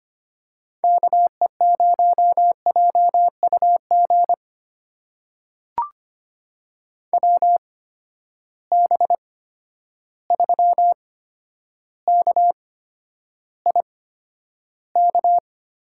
Morse code, heard as a keyed tone, about 25 wpm, spaced as for 11 wpm, 700 Hz. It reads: KE0JUG WB3KIK
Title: USA Call Signs - 6 Characters - 8x Character Spacing - 25wpm